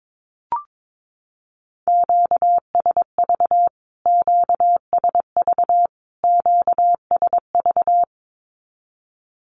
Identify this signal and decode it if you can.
Morse code, heard as a keyed tone, about 22 wpm, 700 Hz. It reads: QSV QSV QSV
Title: Q-Codes Repeated 3x 22wpm